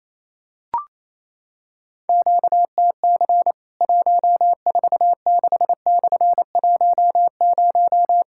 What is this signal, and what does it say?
Morse code, heard as a keyed tone, about 28 wpm, 700 Hz. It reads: QTC 146/10
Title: Worked All Europe DX Contest - QTC Exchange - 28wpm